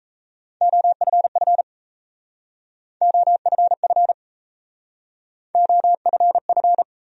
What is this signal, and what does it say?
Morse code, heard as a keyed tone, about 37 wpm, 700 Hz. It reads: OFF OFF OFF